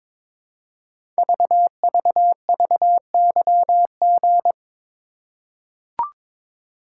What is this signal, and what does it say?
Morse code, heard as a keyed tone, about 22 wpm, 700 Hz. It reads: VVVYG
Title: Code Groups - 5 Letters - 22wpm